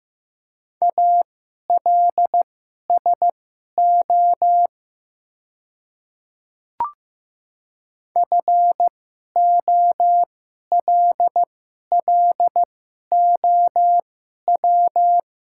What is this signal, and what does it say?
Morse code, heard as a keyed tone, about 15 wpm, spaced as for 11 wpm, 700 Hz. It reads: ALSO FOLLOW
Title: 2x Character Spacing - Top 200 Words - 15wpm